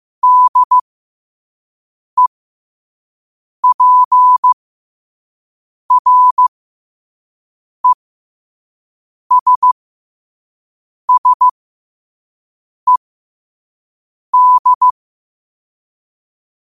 Morse code, heard as a keyed tone, about 15 wpm, spaced as for 5 wpm, 1000 Hz. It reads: DEPRESSED
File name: qotdfiles / 5wpm